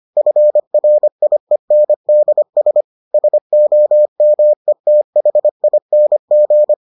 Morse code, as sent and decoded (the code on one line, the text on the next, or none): ..-. .-. .. -. -.. ... / ... --- -- . - .... .. -. --.
FRIENDS SOMETHING